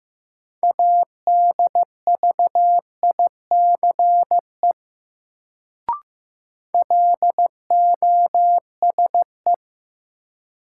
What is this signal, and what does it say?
Morse code, heard as a keyed tone, about 15 wpm, 700 Hz. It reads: ADVICE LOSE
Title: Sentences from Top 1100 Words - New Word Drill - 15wpm